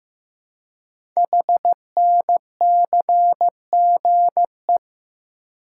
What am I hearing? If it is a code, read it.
Morse code, heard as a keyed tone, about 15 wpm, 700 Hz. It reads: HNCGE
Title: Code Groups - 5 Letters - 15wpm